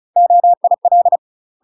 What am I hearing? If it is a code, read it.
Morse code, heard as a keyed tone, about 35 wpm, 700 Hz. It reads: OIL
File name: UK MASTER QSOs TRAINING/35WPM